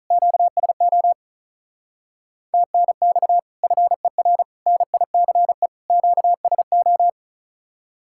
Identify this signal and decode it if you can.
Morse code, heard as a keyed tone, about 46 wpm, 700 Hz. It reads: QSO TNX FER NICE QSO